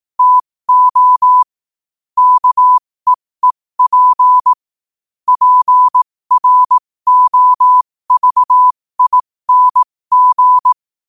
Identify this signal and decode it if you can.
Morse code, heard as a keyed tone, about 18 wpm, spaced as for 15 wpm, 1000 Hz. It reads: TO KEEP PROVING
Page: New quotes every day in morse code at 15 Words per minute.